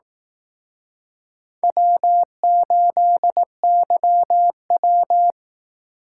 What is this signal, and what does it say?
Morse code, heard as a keyed tone, about 20 wpm, 700 Hz. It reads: W8YW